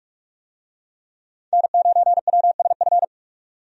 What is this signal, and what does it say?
Morse code, heard as a keyed tone, about 46 wpm, 700 Hz. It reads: N9WSF
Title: USA Call Signs - 5 Characters - 45wpm